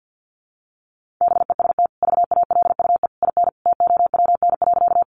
Morse code, heard as a keyed tone, about 50 wpm, 700 Hz. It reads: BEST VALUE IN TOWN.
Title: Sentences from Top 600 Words 50wpm